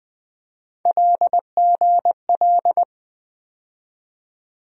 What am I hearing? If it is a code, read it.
Morse code, heard as a keyed tone, about 20 wpm, 700 Hz. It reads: LGL
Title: USA Call Sign Postfixes - 3 Letters 20wpm